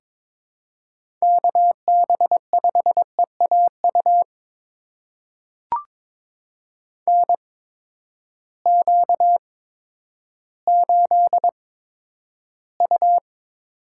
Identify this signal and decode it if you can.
Morse code, heard as a keyed tone, about 22 wpm, spaced as for 11 wpm, 700 Hz. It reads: KB5EAU NQ8U
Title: USA Call Signs - 8x Character Spacing - 22wpm